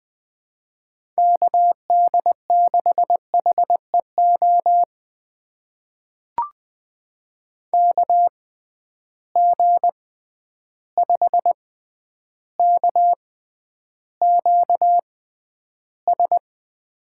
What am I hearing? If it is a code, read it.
Morse code, heard as a keyed tone, about 20 wpm, spaced as for 10 wpm, 700 Hz. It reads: KD6HEO KG5KQS
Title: USA Call Signs - 6 Characters - 6x Character Spacing - 20wpm